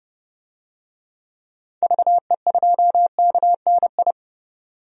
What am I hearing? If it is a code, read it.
Morse code, heard as a keyed tone, about 32 wpm, 700 Hz. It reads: VE2KNI
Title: DXCC-100 Call Signs - 6 Characters - 30wpm